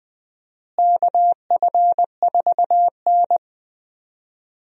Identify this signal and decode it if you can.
Morse code, heard as a keyed tone, about 20 wpm, 700 Hz. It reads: KF4N
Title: USA Call Signs - 4 Characters - Repeated 2x - 20wpm